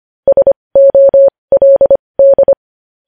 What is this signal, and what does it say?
Morse code, heard as a keyed tone, about 25 wpm, spaced as for 20 wpm, 550 Hz. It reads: SOLD
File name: podcast / News headlines-1x-25x20